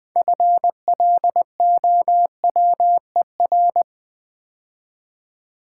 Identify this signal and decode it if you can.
Morse code, heard as a keyed tone, about 20 wpm, 700 Hz. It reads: FLOWER
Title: Build A Word - 3 Letters At A Time - 6-Letter Words - 20wpm